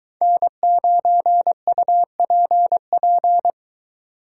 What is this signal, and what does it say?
Morse code, heard as a keyed tone, about 23 wpm, 700 Hz. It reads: N9UPP